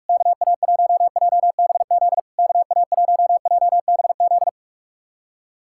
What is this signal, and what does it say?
Morse code, heard as a keyed tone, about 44 wpm, 690 Hz. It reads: KA1JBZ KA1JBZ
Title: SOTA QSO Practice 45wpm